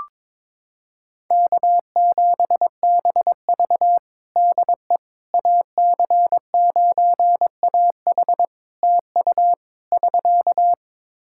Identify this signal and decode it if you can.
Morse code, heard as a keyed tone, about 22 wpm, 700 Hz. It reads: K7BV DE AC9AH TU <SK>
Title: Practice QSO - Exchange 4 - Ending - 22wpm